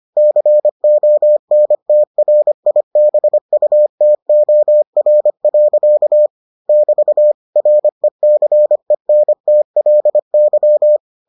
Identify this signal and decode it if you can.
Morse code, heard as a keyed tone, about 25 wpm, 600 Hz. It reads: CONTRIBUTOR. <BT> RECENTLY